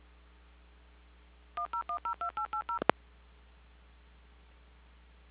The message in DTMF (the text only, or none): 10102880